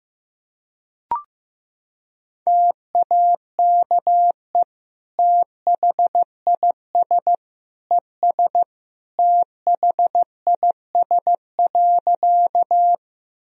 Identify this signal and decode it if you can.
Morse code, heard as a keyed tone, about 15 wpm, 700 Hz. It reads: TAKE THIS ES THIS.
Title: Sentences from Top 200 Words and Abbreviations 15wpm